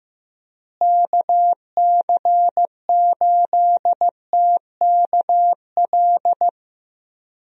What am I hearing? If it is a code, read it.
Morse code, heard as a keyed tone, about 16 wpm, 700 Hz. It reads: KC8TKL
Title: USA Call Signs - 6 Characters - Repeated 3x - 15wpm